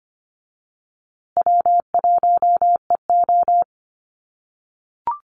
Morse code, heard as a keyed tone, about 25 wpm, 700 Hz. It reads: W1EO
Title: USA Call Signs Repeated 3x 25wpm